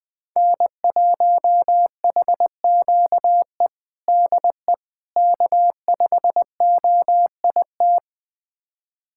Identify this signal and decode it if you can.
Morse code, heard as a keyed tone, about 20 wpm, 700 Hz. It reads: N1HQE DE K5OIT
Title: USA Callsign DE Callsign - 5 Characters - 20wpm